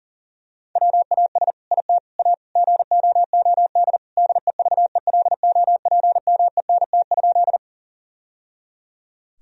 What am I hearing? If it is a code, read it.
Morse code, heard as a keyed tone, about 37 wpm, 700 Hz. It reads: WAS IT A GOOD DEVELOPMENT?
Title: Sentences from Top 1000 Words - Review Part 4 - 40wpm